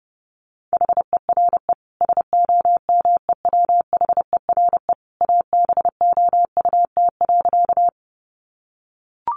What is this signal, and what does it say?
Morse code, heard as a keyed tone, about 28 wpm, 700 Hz. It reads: HERE SOMEWHERE ABOUT.
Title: Sentences from Top 1400 Words 30wpm